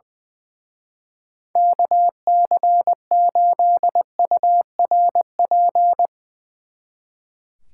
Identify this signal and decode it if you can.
Morse code, heard as a keyed tone, about 22 wpm, 700 Hz. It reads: KC8URP
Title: Speed Racing - USA 2x3 Call Signs 20wpm